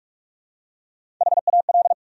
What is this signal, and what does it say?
Morse code, heard as a keyed tone, about 46 wpm, 700 Hz. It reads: HAL
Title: USA Common Names - Max 3 Letters - 45wpm